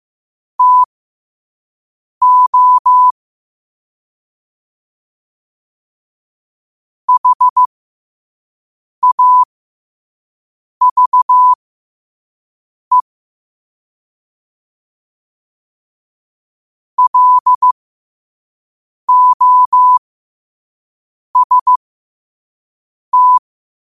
Morse code, heard as a keyed tone, about 15 wpm, spaced as for 5 wpm, 1000 Hz. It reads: TO HAVE LOST